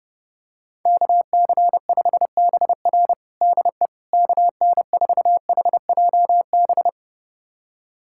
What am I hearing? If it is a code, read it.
Morse code, heard as a keyed tone, about 32 wpm, 700 Hz. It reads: KC5BR DE KN4HJB
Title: USA Callsign DE Callsign 30wpm